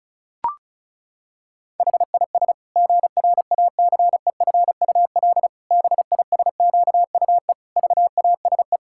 Morse code, heard as a keyed tone, about 36 wpm, 700 Hz. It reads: HIS GRACEFUL BISQUE VASE
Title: Pangram Challenge v2 - 35wpm